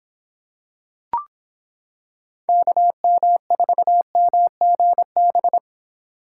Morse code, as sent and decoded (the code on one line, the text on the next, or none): -.- -- ....- -- --. -...
KM4MGB